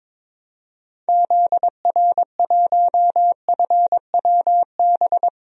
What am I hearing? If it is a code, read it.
Morse code, heard as a keyed tone, about 22 wpm, 700 Hz. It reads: ZR1FWB
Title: DXCC-100 Call Signs - 6 Characters - 22wpm